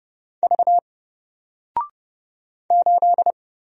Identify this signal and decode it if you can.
Morse code, heard as a keyed tone, about 30 wpm, 700 Hz. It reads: V 8
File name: Single-Letter-Number-30wpm